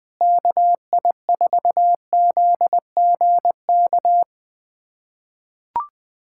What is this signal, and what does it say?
Morse code, heard as a keyed tone, about 21 wpm, 700 Hz. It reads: KI4ZGK
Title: USA Callsign DE Callsign - 6 Characters - 20wpm